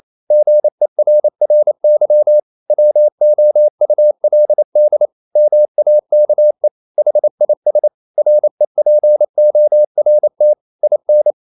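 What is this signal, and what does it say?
Morse code, heard as a keyed tone, about 28 wpm, 600 Hz. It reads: GERRY WOULD MAKE HIS REPORT IN
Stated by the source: Morse code practice at 28 wpm, text of Ken Follett's novel Never in English, from CW med Gnister.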